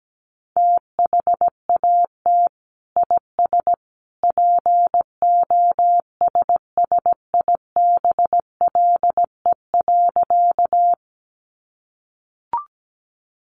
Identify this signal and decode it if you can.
Morse code, heard as a keyed tone, about 17 wpm, 700 Hz. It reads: THAT IS POSSIBLE.
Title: Sentences from Top 500 Words 17wpm